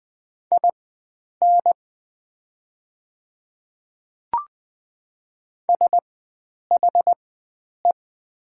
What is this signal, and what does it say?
Morse code, heard as a keyed tone, about 20 wpm, spaced as for 9 wpm, 700 Hz. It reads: IN SHE
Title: 4x Character Spacing - Top 100 Words - 20wpm